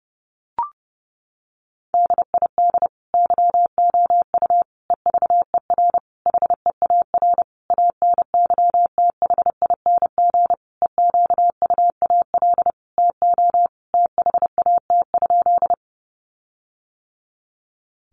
Morse code, heard as a keyed tone, about 30 wpm, 700 Hz. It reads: DID YOU EVER HEAR ANYTHING EQUAL TO THAT?
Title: Sentences from Top 1400 Words 30wpm